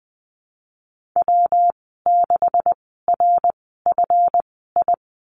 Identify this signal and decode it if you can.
Morse code, heard as a keyed tone, about 20 wpm, spaced as for 15 wpm, 700 Hz. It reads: W6RFI
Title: 154 Lesson 25g - Letter p - Call Signs 20wpm